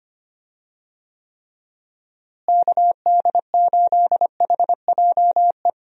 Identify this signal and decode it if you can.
Morse code, heard as a keyed tone, about 25 wpm, 700 Hz. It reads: KD8HJE
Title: USA Call Signs - 6 Characters - Repeated 2x - 25wpm